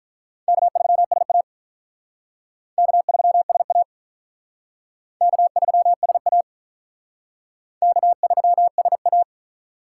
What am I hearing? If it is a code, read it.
Morse code, heard as a keyed tone, about 44 wpm, 700 Hz. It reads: K3SA K3SA K3SA K3SA